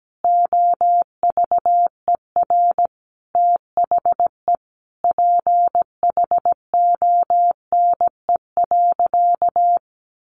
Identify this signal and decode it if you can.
Morse code, heard as a keyed tone, about 17 wpm, 700 Hz. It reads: OVER THE PHONE.
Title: Sentences from Top 1100 Words 17wpm